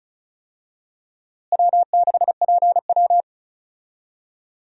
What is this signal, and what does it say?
Morse code, heard as a keyed tone, about 36 wpm, 700 Hz. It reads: W6PW
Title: USA Call Signs - 4 Characters - Repeated 2x - 35wpm